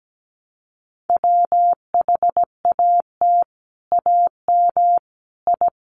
Morse code, heard as a keyed tone, about 17 wpm, 700 Hz. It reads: WHAT AM I
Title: Sets of 5 Words Encore2 17wpm